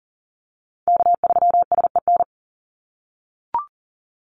Morse code, heard as a keyed tone, about 37 wpm, 700 Hz. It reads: K3SEN
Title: Call Signs - 40wpm